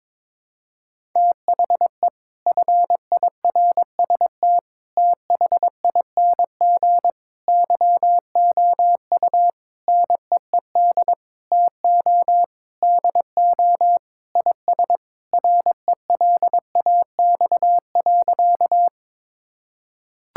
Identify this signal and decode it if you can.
Morse code, heard as a keyed tone, about 22 wpm, 700 Hz. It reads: THE FIRST THING YOU NEED TO DO IS RELAX.
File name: practice / 22wpm / Sentences with 8-Word Phrases - 22wpm